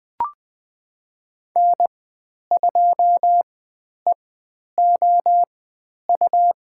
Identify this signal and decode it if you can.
Morse code, heard as a keyed tone, about 20 wpm, spaced as for 10 wpm, 700 Hz. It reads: N2EOU
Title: 102 Lesson 17g - Letter u - Call Signs 20wpm